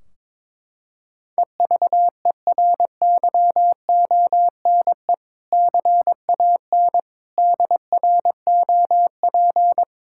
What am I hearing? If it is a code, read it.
Morse code, heard as a keyed tone, about 22 wpm, 700 Hz. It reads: EVERYONE CAN DROP
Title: Sentences from Top 1600 Words 22wpm